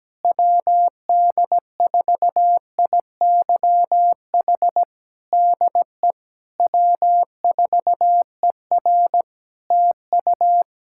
Morse code, heard as a keyed tone, about 17 wpm, 700 Hz. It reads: WD4IYH DE W4ER TU
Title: Practice QSO - Exchange 4 - Ending - 17wpm